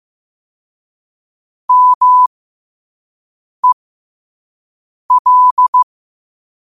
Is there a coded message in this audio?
Yes: Morse code from 1.7 to 5.8 s, heard as a keyed tone, about 15 wpm, spaced as for 5 wpm, 1000 Hz. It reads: MEL